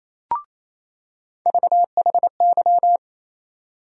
Morse code, heard as a keyed tone, about 28 wpm, 700 Hz. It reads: VHY
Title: USA Call Sign Postfixes - 3 Letters 28wpm